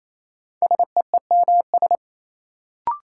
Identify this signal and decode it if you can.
Morse code, heard as a keyed tone, about 28 wpm, 700 Hz. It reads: SEEMS
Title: Sets of 3 Words - New Word Drill - 28wpm